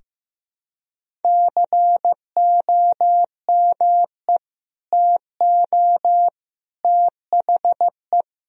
Morse code, heard as a keyed tone, about 15 wpm, 700 Hz. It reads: COME TO THE
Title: Sentences from Top 1000 Words - Review Part 1 - 15wpm